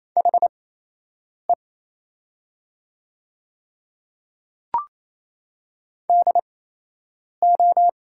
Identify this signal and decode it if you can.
Morse code, heard as a keyed tone, about 28 wpm, spaced as for 8 wpm, 700 Hz. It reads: HE DO
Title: 8x Character Spacing - Top 100 Words - 28wpm